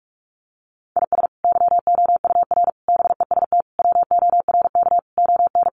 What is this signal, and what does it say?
Morse code, heard as a keyed tone, about 44 wpm, 700 Hz. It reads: IS YOUR BEST WORK ON
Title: Sentences from Top 1200 Words 45wpm